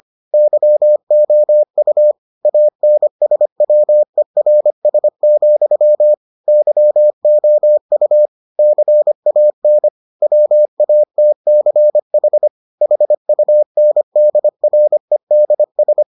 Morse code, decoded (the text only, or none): YOU ANSWERS, YOU CAN WATCH HUNDREDS